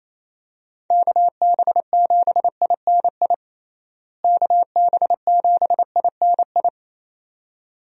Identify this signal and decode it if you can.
Morse code, heard as a keyed tone, about 28 wpm, 700 Hz. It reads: KB7INI KB7INI
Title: USA Call Signs - 6 Characters - Repeated 2x - 28wpm